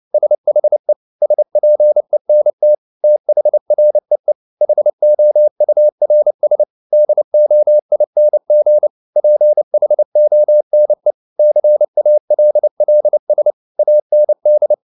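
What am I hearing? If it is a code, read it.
Morse code, heard as a keyed tone, about 29 wpm, 600 Hz. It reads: SHE SPENT THREE HOURS DOING PHONE CALLS AND
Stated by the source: Morse code at 29 wpm, sending Ken Follett's novel Never in English, from CW med Gnister.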